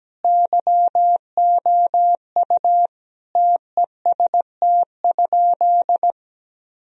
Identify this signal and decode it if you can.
Morse code, heard as a keyed tone, about 17 wpm, 700 Hz. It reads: YOU TEST?
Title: Sentences from Top 1000 Words - Review Part 3 - 17wpm